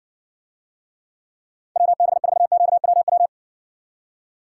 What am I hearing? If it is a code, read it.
Morse code, heard as a keyed tone, about 64 wpm, 700 Hz. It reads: WB4QWW